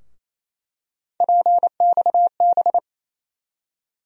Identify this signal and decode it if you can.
Morse code, heard as a keyed tone, about 28 wpm, 700 Hz. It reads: PXB
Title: USA Call Sign Postfixes - 3 Letters 28wpm